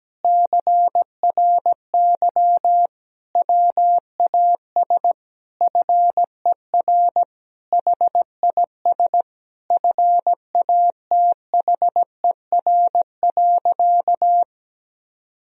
Morse code, heard as a keyed tone, about 17 wpm, 700 Hz. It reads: CRY WAS FER HIS FATHER.
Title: Sentences from Top 500 Words and Abbreviations 17wpm